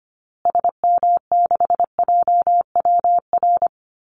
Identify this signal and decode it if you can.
Morse code, heard as a keyed tone, about 25 wpm, 700 Hz. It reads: SM6JWR
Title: Call Signs Repeated 3x - 25wpm